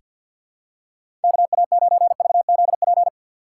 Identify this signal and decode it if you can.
Morse code, heard as a keyed tone, about 50 wpm, 700 Hz. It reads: KA9VZP
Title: USA Call Signs - 6 Characters - 4x Character Spacing - 50wpm